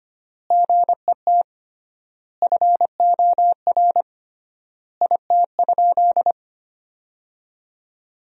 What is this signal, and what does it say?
Morse code, heard as a keyed tone, about 25 wpm, 700 Hz. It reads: GET FOR IT?
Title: 3x Word-Spacing - Sentences from Top 400 Words - 25wpm